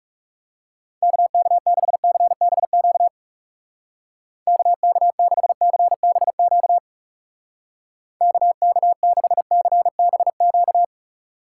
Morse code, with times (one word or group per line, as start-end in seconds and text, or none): 1.0-3.1 KK6CBQ
4.5-6.8 KK6CBQ
8.2-10.8 KK6CBQ